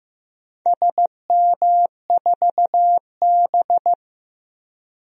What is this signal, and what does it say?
Morse code, heard as a keyed tone, about 15 wpm, 700 Hz. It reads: SM4B
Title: DXCC-100 Call Signs - 4 Characters - Repeated 3x - 15wpm